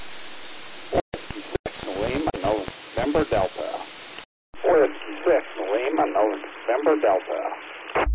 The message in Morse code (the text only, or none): none